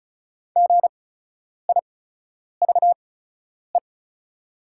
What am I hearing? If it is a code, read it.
Morse code, heard as a keyed tone, about 36 wpm, spaced as for 10 wpm, 700 Hz. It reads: GIVE
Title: 8x Character Spacing - Top 200 Words - 35wpm